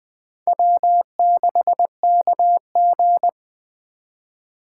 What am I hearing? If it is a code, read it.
Morse code, heard as a keyed tone, about 20 wpm, 700 Hz. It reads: W6KG